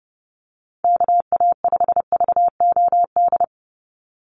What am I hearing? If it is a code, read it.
Morse code, heard as a keyed tone, about 30 wpm, 700 Hz. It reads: KA5VOD
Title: USA Call Signs 30wpm